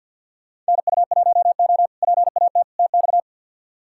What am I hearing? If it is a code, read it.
Morse code, heard as a keyed tone, about 49 wpm, 700 Hz. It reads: NU1O PAT TX
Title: K1USN SST QSO Practice - 50wpm